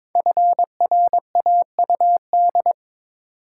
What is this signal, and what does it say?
Morse code, heard as a keyed tone, about 23 wpm, 700 Hz. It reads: FRAUD
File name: 5 Letter Words - First 3 Letters Given - 22wpm